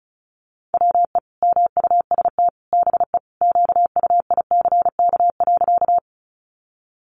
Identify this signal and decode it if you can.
Morse code, heard as a keyed tone, about 34 wpm, 700 Hz. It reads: WE MUST BE QUICK.
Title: Sentences from Top 500 Words 35wpm